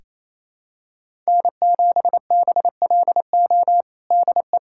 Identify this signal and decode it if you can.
Morse code, heard as a keyed tone, about 28 wpm, 700 Hz. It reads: N7BLO DE
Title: USA Callsign DE Callsign - 5 Characters - 28wpm